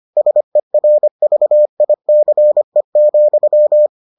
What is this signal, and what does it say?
Morse code, heard as a keyed tone, about 25 wpm, 600 Hz. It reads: SERVICE,